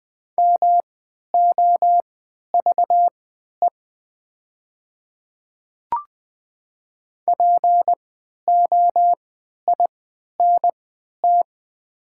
Morse code, heard as a keyed tone, about 20 wpm, spaced as for 11 wpm, 700 Hz. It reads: MOVE POINT
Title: 3x Character Spacing - Top 500 Words - 20wpm